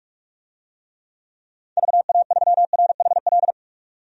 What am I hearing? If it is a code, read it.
Morse code, heard as a keyed tone, about 45 wpm, 690 Hz. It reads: VA3RHL
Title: DXCC-100 Call Signs - 6 Characters - 45wpm